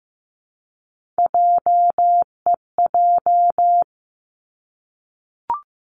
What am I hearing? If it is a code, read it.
Morse code, heard as a keyed tone, about 15 wpm, 700 Hz. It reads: JEJ
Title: Three Challenging Letter-Number Combinations 15wpm